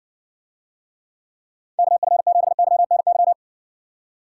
Morse code, heard as a keyed tone, about 60 wpm, 700 Hz. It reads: KF7YNY